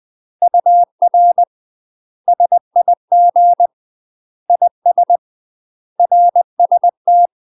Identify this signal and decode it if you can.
Morse code, heard as a keyed tone, about 20 wpm, 700 Hz. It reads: UR SIG IS RST
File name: UK MASTER QSOs TRAINING / 20WPM / UK MASTER QSOs 0061